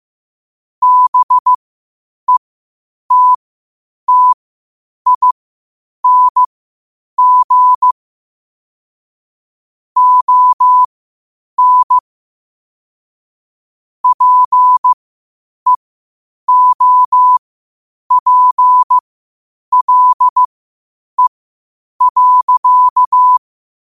Morse code, heard as a keyed tone, about 15 wpm, spaced as for 8 wpm, 1000 Hz. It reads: BETTING ON PEOPLE.
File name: qotdfiles / 8wpm